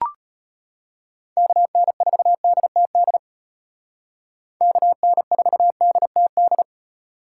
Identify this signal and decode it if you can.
Morse code, heard as a keyed tone, about 33 wpm, 700 Hz. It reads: KN4DTD KN4DTD